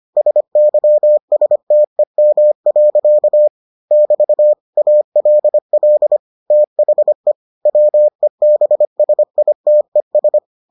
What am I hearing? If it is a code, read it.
Morse code, heard as a keyed tone, about 25 wpm, 600 Hz. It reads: SYSTEM. <BT> ALL THE WEBSITES